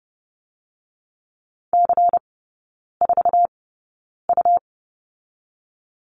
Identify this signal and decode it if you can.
Morse code, heard as a keyed tone, about 30 wpm, spaced as for 9 wpm, 690 Hz. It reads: C4U
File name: Lesson 17e - Letter u - Three Characters 30wpm